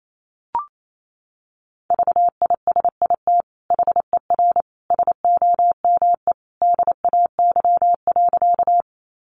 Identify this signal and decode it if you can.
Morse code, heard as a keyed tone, about 28 wpm, 700 Hz. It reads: VISIT HER SOME DAY.
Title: Sentences from Top 900 Words 28wpm